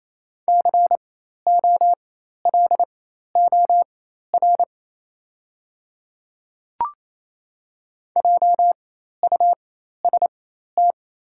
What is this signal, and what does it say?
Morse code, heard as a keyed tone, about 28 wpm, spaced as for 13 wpm, 700 Hz. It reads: COLOR JUST
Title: 4x Character Spacing - Top 300 Words - 28wpm